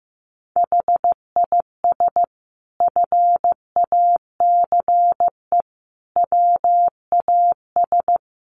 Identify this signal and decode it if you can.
Morse code, heard as a keyed tone, about 15 wpm, 700 Hz. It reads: HIS FACE WAS
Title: Sentences from Top 1200 Words 15wpm